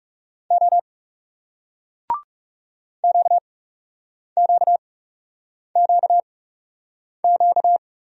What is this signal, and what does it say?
Morse code, heard as a keyed tone, about 38 wpm, spaced as for 8 wpm, 700 Hz. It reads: O QQQQ